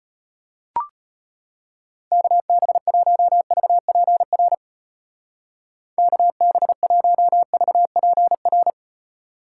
Morse code, heard as a keyed tone, about 31 wpm, 700 Hz. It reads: KB1VPR KB1VPR
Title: Speed Racing - USA 2x3 Call Signs 25wpm